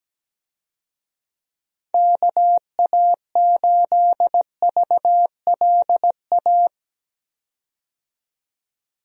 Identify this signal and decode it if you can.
Morse code, heard as a keyed tone, about 17 wpm, 700 Hz. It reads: KA8VLA
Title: USA Call Signs - 6 Characters - 17wpm